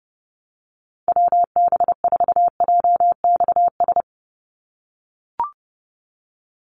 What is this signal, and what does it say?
Morse code, heard as a keyed tone, about 30 wpm, 700 Hz. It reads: WB4JXS
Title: USA Call Signs 30wpm